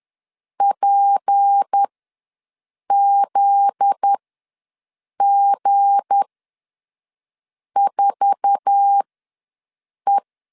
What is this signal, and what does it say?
Morse code, heard as a keyed tone, about 10 wpm, spaced as for 5 wpm, 800 Hz. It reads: PZG4E
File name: LEZIONI / 8 YJ45 / 8 YJ4 ws 10wpm 4020